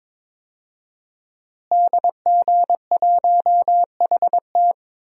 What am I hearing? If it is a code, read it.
Morse code, heard as a keyed tone, about 22 wpm, 700 Hz. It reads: DG1HT
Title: DXCC-100 Call Signs - 5 Characters - Repeated 3x - 22wpm